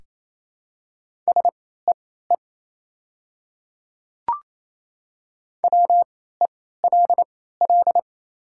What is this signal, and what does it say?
Morse code, heard as a keyed tone, about 28 wpm, spaced as for 16 wpm, 700 Hz. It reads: SEE WELL
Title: 3x Character Spacing - Top 300 Words - 28wpm